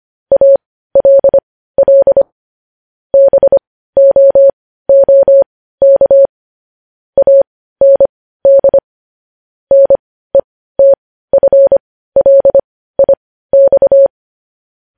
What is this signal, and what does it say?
Morse code, heard as a keyed tone, about 25 wpm, spaced as for 15 wpm, 550 Hz. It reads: ALL BOOK AND NETFLIX
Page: News Headlines in Morse Code at 15 WPM